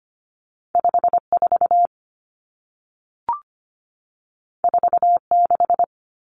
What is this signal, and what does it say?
Morse code, heard as a keyed tone, about 25 wpm, 700 Hz. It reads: 54 46
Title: Numbers - 2-Digit 25wpm